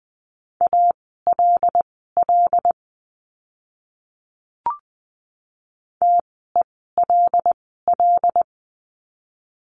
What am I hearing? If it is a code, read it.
Morse code, heard as a keyed tone, about 20 wpm, spaced as for 14 wpm, 700 Hz. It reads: ALL TELL
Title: 192 Lesson 31f - Letter v - Words 20wpm